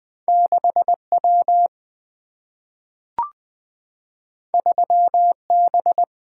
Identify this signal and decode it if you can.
Morse code, heard as a keyed tone, about 20 wpm, 700 Hz. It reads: 6W 3B
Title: Single-Letter Single-Number 20wpm